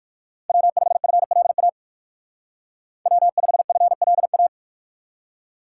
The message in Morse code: W5FLA W5FLA

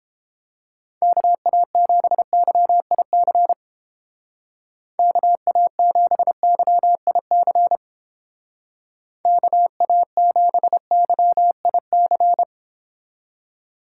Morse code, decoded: KA7YIC KA7YIC KA7YIC

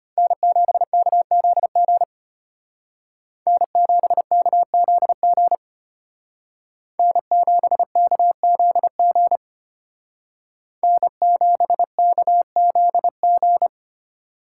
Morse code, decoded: N7KZG N7KZG N7KZG N7KZG

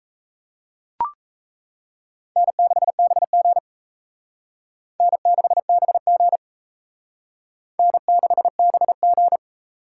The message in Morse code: N6BG N6BG N6BG

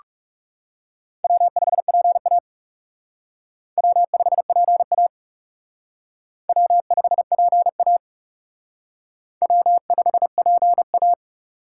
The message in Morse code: W5PA W5PA W5PA W5PA

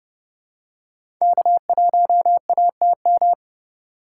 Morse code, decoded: K1ATM

